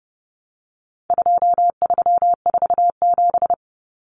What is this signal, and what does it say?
Morse code, heard as a keyed tone, about 30 wpm, 700 Hz. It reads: 2347
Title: Numbers - 4-Digit 30wpm